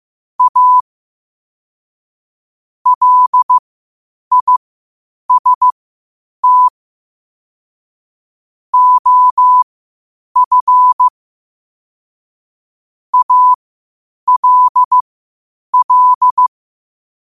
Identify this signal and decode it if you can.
Morse code, heard as a keyed tone, about 15 wpm, spaced as for 8 wpm, 1000 Hz. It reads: A LIST OF ALL
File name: qotdfiles / 8wpm